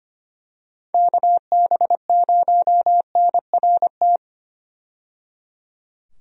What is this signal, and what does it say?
Morse code, heard as a keyed tone, about 25 wpm, 700 Hz. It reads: KB0NRT
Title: USA Call Signs - 6 Characters - 25wpm